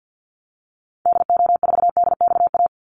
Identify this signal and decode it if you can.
Morse code, heard as a keyed tone, about 50 wpm, 700 Hz. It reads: DO4DXA
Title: Call Signs Repeated 3x - 50wpm